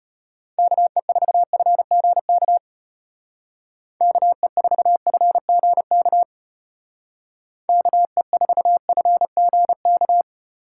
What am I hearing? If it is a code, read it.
Morse code, heard as a keyed tone, about 34 wpm, 700 Hz. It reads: KE4FGK KE4FGK KE4FGK